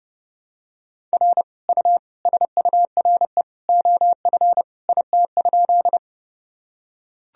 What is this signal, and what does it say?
Morse code, heard as a keyed tone, about 32 wpm, 700 Hz. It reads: R U SURE OF IT?
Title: Sentences from Top 300 Words and Abbreviations 30wpm